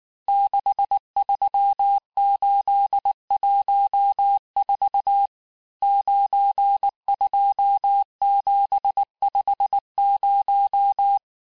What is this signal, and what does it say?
Morse code, heard as a keyed tone, about 19 wpm, 800 Hz. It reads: 63814 92750
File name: cours-du-lundi files / cours-1583--texte-chiffre-18-mpm